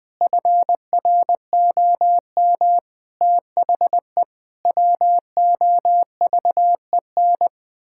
Morse code, heard as a keyed tone, about 21 wpm, 700 Hz. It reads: FROM THE WOVEN
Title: Pangram Challenge v2 - 20wpm